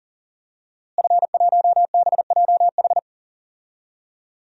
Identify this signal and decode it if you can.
Morse code, heard as a keyed tone, about 40 wpm, 700 Hz. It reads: F1BJH
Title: DXCC-100 Call Signs - 5 Characters - Repeated 2x - 40wpm